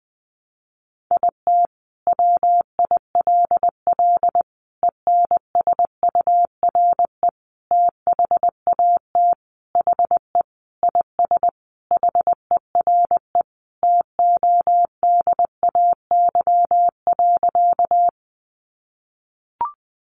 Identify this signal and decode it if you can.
Morse code, heard as a keyed tone, about 20 wpm, 700 Hz. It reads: IT WILL ENSURE THAT HE IS HERE TODAY.
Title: Sentences from Top 1100 Words 20wpm